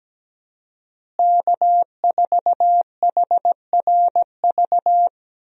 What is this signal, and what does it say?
Morse code, heard as a keyed tone, about 17 wpm, 700 Hz. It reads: K4HRV
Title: USA Call Signs - 5 Characters - Repeated 2x - 17wpm